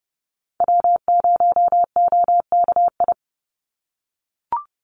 Morse code, heard as a keyed tone, about 30 wpm, 700 Hz. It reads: W0OKI
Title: USA Call Signs 30wpm